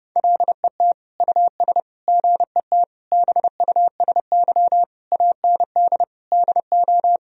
Morse code, heard as a keyed tone, about 30 wpm, 700 Hz. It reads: LET US GET BUSY AND DO
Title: Sentences from Top 1000 Words - Review Part 3 - 30wpm